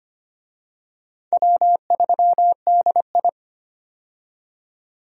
Morse code, heard as a keyed tone, about 25 wpm, 700 Hz. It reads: W3DI
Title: USA Call Signs - 4 Characters - 25wpm